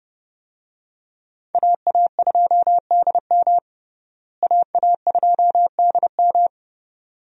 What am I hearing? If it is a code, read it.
Morse code, heard as a keyed tone, about 30 wpm, 700 Hz. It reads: AA2DM AA2DM
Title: USA Call Signs - 5 Characters - Repeated 2x - 30wpm